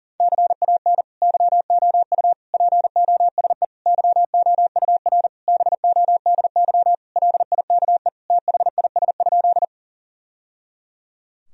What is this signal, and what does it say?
Morse code, heard as a keyed tone, about 37 wpm, 700 Hz. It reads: CAN YOU POSE YOUR BODY LIKE THIS?
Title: Sentences from Top 1000 Words - Review Part 2 - 40wpm